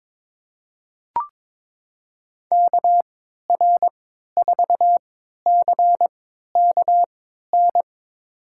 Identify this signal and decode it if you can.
Morse code, heard as a keyed tone, about 22 wpm, spaced as for 12 wpm, 700 Hz. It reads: KR4CKN
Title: USA Call Signs - 6 Characters - 3x Character Spacing - 22wpm